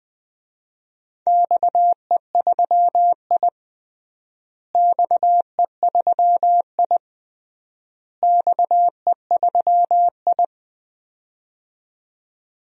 Morse code, heard as a keyed tone, about 21 wpm, 700 Hz. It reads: XE3I XE3I XE3I
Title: DXCC-100 Call Signs - 4 Characters - Repeated 3x - 20wpm